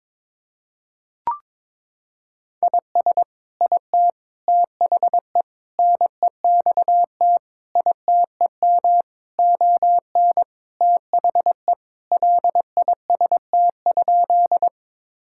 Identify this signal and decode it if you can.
Morse code, heard as a keyed tone, about 21 wpm, 700 Hz. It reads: IS IT THE NEXT ITEM ON THE LIST?
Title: Sentences from Top 1000 Words - Review Part 6 - 22wpm